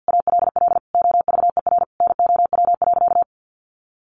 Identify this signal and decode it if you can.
Morse code, heard as a keyed tone, about 50 wpm, 700 Hz. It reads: ALL OVER NOW.
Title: Sentences from Top 1300 Words 50wpm